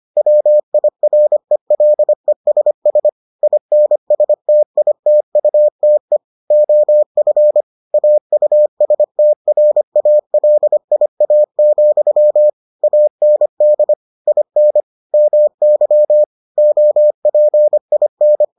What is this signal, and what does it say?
Morse code, heard as a keyed tone, about 25 wpm, 600 Hz. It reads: WIRELESS INSTITUTE OF AUSTRALIA, AND IN MY OPIN